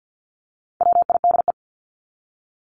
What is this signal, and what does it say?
Morse code, heard as a keyed tone, about 50 wpm, 700 Hz. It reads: WIDE
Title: Top 751-1000 Words 50wpm